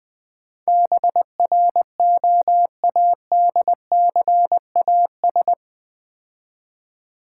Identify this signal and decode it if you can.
Morse code, heard as a keyed tone, about 20 wpm, 700 Hz. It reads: BROADCAS
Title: Build A Word - 4 Letters At A Time - 11-Letter Words - 20wpm